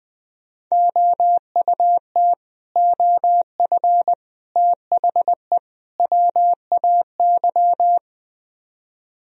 Morse code, heard as a keyed tone, about 20 wpm, 700 Hz. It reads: OUT OF THE WAY
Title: Speed-Racing - Sets of 4 Words 17wpm